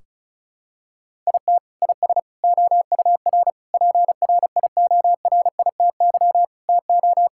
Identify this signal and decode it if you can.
Morse code, heard as a keyed tone, about 34 wpm, 700 Hz. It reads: IT IS OUR PRIORITY TO
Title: Sentences from Top 1600 Words 35wpm